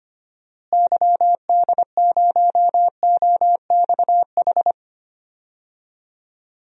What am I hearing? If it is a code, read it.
Morse code, heard as a keyed tone, about 25 wpm, 700 Hz. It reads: YD0OXH
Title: DXCC-100 Call Signs - 6 Characters - Repeated 2x - 25wpm